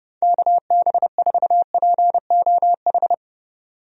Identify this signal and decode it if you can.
Morse code, heard as a keyed tone, about 30 wpm, 700 Hz. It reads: KB4POH
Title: USA Call Signs - 6 Characters - Repeated 3x - 30wpm